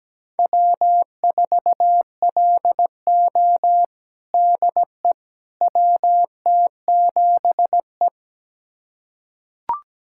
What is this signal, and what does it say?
Morse code, heard as a keyed tone, about 17 wpm, 700 Hz. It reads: W4LO DE WT7E
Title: USA Callsign DE Callsign - 4 Characters - 17wpm